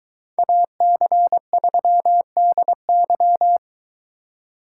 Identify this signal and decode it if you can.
Morse code, heard as a keyed tone, about 23 wpm, 700 Hz. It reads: AC3DY